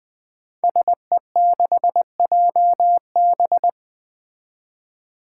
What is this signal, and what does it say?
Morse code, heard as a keyed tone, about 20 wpm, 700 Hz. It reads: SE6JB
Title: Five Challenging Letter-Number Combinations 20wpm